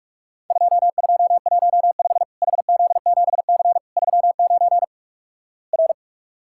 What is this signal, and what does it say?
Morse code, heard as a keyed tone, about 47 wpm, 700 Hz. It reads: 2215 HZ7C 39 R
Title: Worked All Europe DX Contest - QTC Exchange - 45wpm